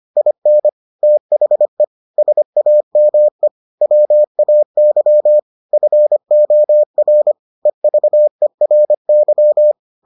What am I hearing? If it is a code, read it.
Morse code, heard as a keyed tone, about 25 wpm, 600 Hz. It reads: IN THE SAME WAY FOR EVERY